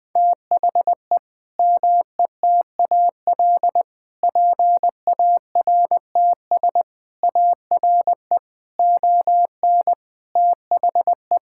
Morse code, heard as a keyed tone, about 20 wpm, 700 Hz. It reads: THE METAL PARTS ARE ON THE
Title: Sentences from Top 1000 Words - Review Part 6 - 20wpm